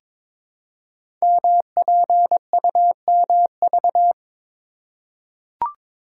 Morse code, heard as a keyed tone, about 22 wpm, 700 Hz. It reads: MPUMV
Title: Code Groups - 5 Letters - 22wpm